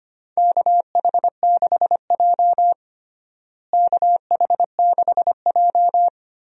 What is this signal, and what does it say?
Morse code, heard as a keyed tone, about 25 wpm, 700 Hz. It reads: KH6J KH6J
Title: USA Call Signs - 4 Characters - Repeated 3x - 25wpm